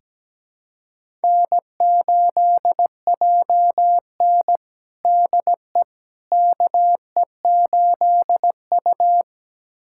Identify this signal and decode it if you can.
Morse code, heard as a keyed tone, about 17 wpm, 700 Hz. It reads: N8JN DE KE8U
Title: USA Callsign DE Callsign - 4 Characters - 17wpm